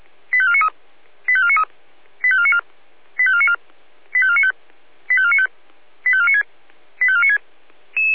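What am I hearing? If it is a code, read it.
Five-tone selective call (ZVEI), heard as quick runs of five tones, tones of about 70 ms. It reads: 85493 85493 85494 85494 85495 85495 85496 85496
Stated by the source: ZVEI five-tone call received on SDR.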